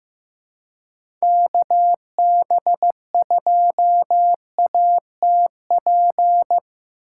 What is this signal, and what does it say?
Morse code, heard as a keyed tone, about 15 wpm, 700 Hz. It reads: KB2ATP
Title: USA Call Signs - 4x Character Spacing - 15wpm